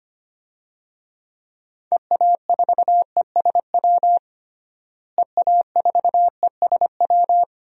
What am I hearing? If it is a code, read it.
Morse code, heard as a keyed tone, about 25 wpm, 700 Hz. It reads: EA4ESW EA4ESW
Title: DXCC-100 Call Signs - 6 Characters - Repeated 2x - 25wpm